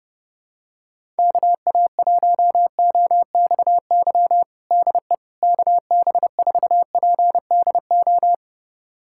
Morse code, heard as a keyed tone, about 30 wpm, 700 Hz. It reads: KA1OXY DE KB4PDO
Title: USA Callsign DE Callsign - 6 Characters - 30wpm